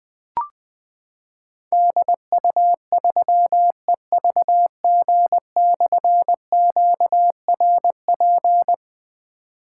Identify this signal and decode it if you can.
Morse code, heard as a keyed tone, about 20 wpm, 700 Hz. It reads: DU3EVG/QRP
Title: Difficult Call Signs 20wpm